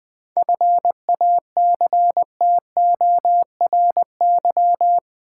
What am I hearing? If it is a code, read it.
Morse code, heard as a keyed tone, about 20 wpm, 700 Hz. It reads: FACTORY
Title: 7 Letter Words - First 3 Letters Given - 20wpm